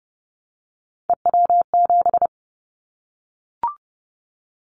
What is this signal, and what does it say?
Morse code, heard as a keyed tone, about 30 wpm, 700 Hz. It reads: EW7
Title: Callsign Prefixes for Top 100 DXCC - 30wpm